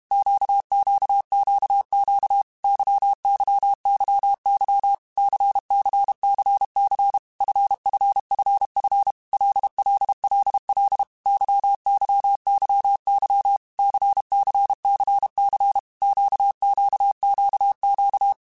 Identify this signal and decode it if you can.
Morse code, heard as a keyed tone, about 32 wpm, 800 Hz. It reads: QQQQ YYYY CCCC FFFF LLLL YYYY CCCC QQQQ
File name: Groups@30 / Groups@30 mp3s / LFCYQ@30